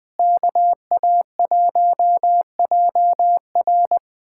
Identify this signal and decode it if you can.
Morse code, heard as a keyed tone, about 20 wpm, 700 Hz. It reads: KA1JR
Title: USA Callsign DE Callsign - 5 Characters - 20wpm